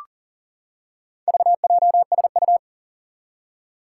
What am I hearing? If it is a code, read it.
Morse code, heard as a keyed tone, about 40 wpm, 700 Hz. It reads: VJSU
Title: Four Challenging Letter-Number Combinations 40wpm